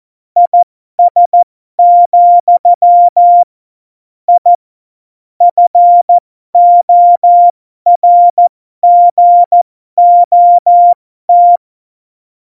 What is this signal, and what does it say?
Morse code, heard as a keyed tone, about 14 wpm, spaced as for 12 wpm, 700 Hz. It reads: IS, I FORGOT